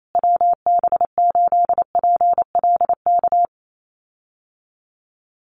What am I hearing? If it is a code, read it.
Morse code, heard as a keyed tone, about 28 wpm, 700 Hz. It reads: WB8PLK
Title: USA Call Signs Repeated 2x 28wpm